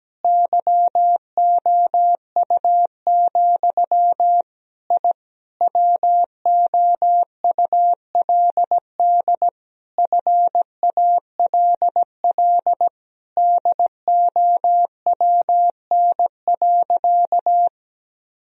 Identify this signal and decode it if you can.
Morse code, heard as a keyed tone, about 17 wpm, 700 Hz. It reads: YOU, I WOULD FALL DOWN.
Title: Sentences from Top 1000 Words - Review Part 2 - 17wpm